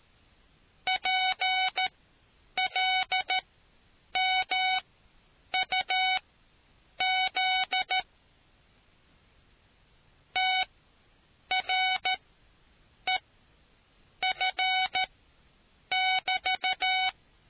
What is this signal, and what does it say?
Morse code, heard as a keyed tone, about 13 wpm, spaced as for 7 wpm, 750 Hz. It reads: PLMUZ TREF <BT>